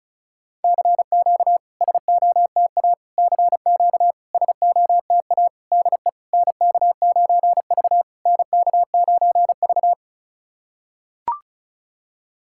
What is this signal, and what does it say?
Morse code, heard as a keyed tone, about 36 wpm, 700 Hz. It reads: CQ SOTA CQ SOTA DE NK9V NK9V
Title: SOTA QSO Practice 35wpm